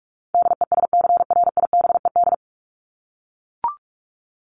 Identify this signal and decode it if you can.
Morse code, heard as a keyed tone, about 44 wpm, 700 Hz. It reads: DESCRIBED
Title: ED Suffix - Top 100 Words with Base Words 45wpm